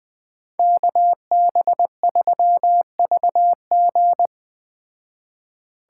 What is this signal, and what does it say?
Morse code, heard as a keyed tone, about 21 wpm, 700 Hz. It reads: KB3VG
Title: Speed Racing - USA 2x2 Call Signs 17wpm